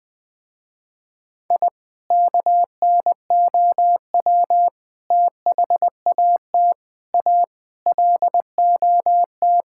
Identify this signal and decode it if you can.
Morse code, heard as a keyed tone, about 20 wpm, 700 Hz. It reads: I KNOW THAT A LOT
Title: Sentences with 5-Word Phrases - Spoken Beginning - 20wpm